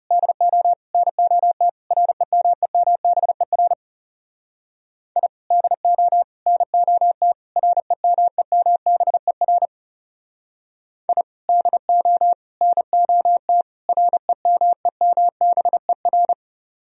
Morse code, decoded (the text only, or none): DO NOT REMEMBER I DO NOT REMEMBER I DO NOT REMEMBER